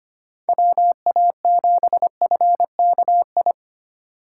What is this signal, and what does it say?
Morse code, heard as a keyed tone, about 28 wpm, 700 Hz. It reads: WA7FKI
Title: Speed Racing - USA 2x3 Call Signs 25wpm